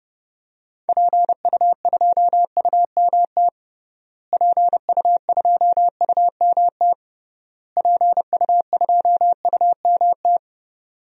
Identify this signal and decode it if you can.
Morse code, heard as a keyed tone, about 30 wpm, 700 Hz. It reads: PU2UMT PU2UMT PU2UMT
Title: DXCC-100 Call Signs - 6 Characters - Repeated 3x - 30wpm